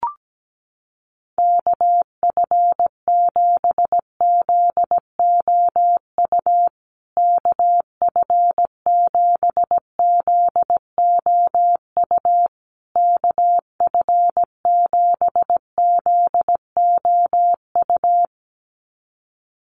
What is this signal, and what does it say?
Morse code, heard as a keyed tone, about 17 wpm, 700 Hz. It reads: KF7ZOU KF7ZOU KF7ZOU
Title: USA Call Signs Repeated 3x 17wpm